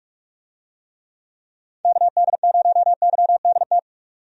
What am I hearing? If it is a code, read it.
Morse code, heard as a keyed tone, about 46 wpm, 700 Hz. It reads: KD0YDT
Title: USA Call Signs - 6 Characters - 45wpm